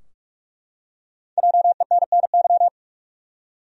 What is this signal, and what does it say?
Morse code, heard as a keyed tone, about 43 wpm, 700 Hz. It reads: JENNY
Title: USA Common Names - Max 5 Letters - 45wpm